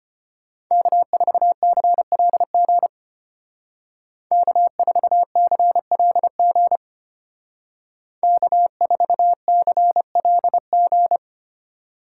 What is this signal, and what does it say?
Morse code, heard as a keyed tone, about 33 wpm, 700 Hz. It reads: K4CLG K4CLG K4CLG